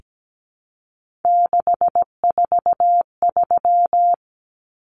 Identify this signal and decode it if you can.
Morse code, heard as a keyed tone, about 17 wpm, 700 Hz. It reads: 643
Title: Numbers - 3-Digit 17wpm